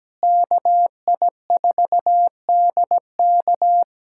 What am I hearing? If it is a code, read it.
Morse code, heard as a keyed tone, about 17 wpm, 700 Hz. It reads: KI4DK
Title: K1USN SST QSO Practice - 17wpm